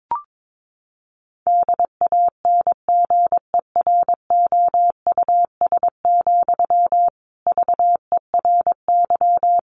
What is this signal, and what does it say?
Morse code, heard as a keyed tone, about 22 wpm, 700 Hz. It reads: DANGEROUS, VERY
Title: Sentences from Top 1200 Words 22wpm